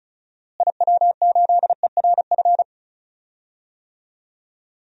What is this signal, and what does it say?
Morse code, heard as a keyed tone, about 36 wpm, 700 Hz. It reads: IW8ERF
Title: DXCC-100 Call Signs - 6 Characters - Repeated 3x - 35wpm